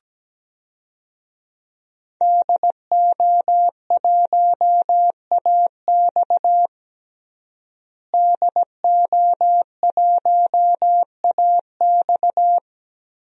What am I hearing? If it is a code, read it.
Morse code, heard as a keyed tone, about 17 wpm, 700 Hz. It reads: DO1AX DO1AX
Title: DXCC-100 Call Signs - 5 Characters - Repeated 3x - 17wpm